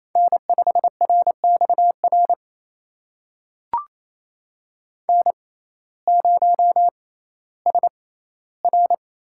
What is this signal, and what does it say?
Morse code, heard as a keyed tone, about 28 wpm, spaced as for 15 wpm, 700 Hz. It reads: N5RXR N0SR
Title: USA Call Signs - 5 Characters - 6x Character Spacing - 28wpm